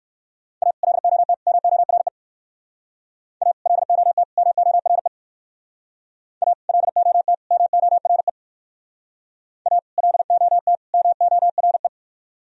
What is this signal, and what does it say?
Morse code, heard as a keyed tone, about 57 wpm, 700 Hz. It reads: A LOT MORE A LOT MORE A LOT MORE A LOT MORE